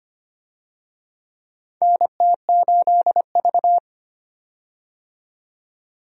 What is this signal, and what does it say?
Morse code, heard as a keyed tone, about 25 wpm, 700 Hz. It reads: NT8V
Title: USA Call Signs - 4 Characters - 25wpm